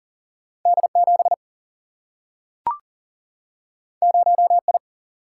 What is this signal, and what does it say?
Morse code, heard as a keyed tone, about 41 wpm, 700 Hz. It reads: D7 0I
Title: Single-Letter Single-Number 40wpm